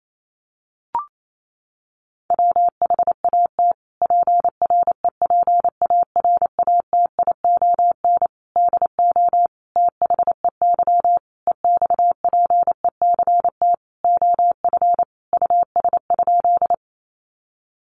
Morse code, heard as a keyed tone, about 28 wpm, 700 Hz. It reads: WHAT PREPARATION DO THEY EXPECT OF US?
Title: Sentences from Top 1400 Words 28wpm